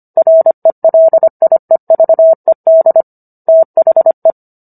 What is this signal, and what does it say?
Morse code, heard as a keyed tone, about 25 wpm, 650 Hz. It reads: RELIEVED THE
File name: Book One / 16-Chapter 16 - 25 WPM